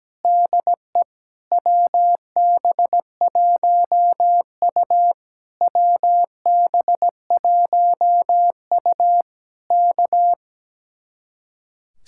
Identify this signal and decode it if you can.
Morse code, heard as a keyed tone, about 17 wpm, 700 Hz. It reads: DE WB1U WB1U K
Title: Speed-Racing Common QSO Phrases - 17wpm